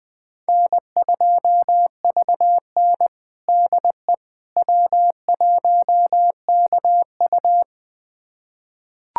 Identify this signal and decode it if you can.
Morse code, heard as a keyed tone, about 20 wpm, 700 Hz. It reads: N2VN DE W1KU
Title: USA Callsign DE Callsign 20wpm